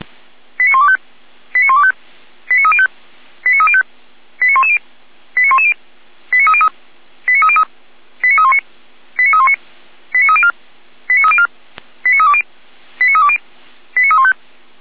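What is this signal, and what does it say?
Five-tone selective call (ZVEI), heard as quick runs of five tones, tones of about 80 ms. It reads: 89125 89125 89305 89305 89110 89110 89303 89303 89319 89319 89304 89304 89320 89320 89315
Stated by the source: ZVEI five-tone selective call received on SDR.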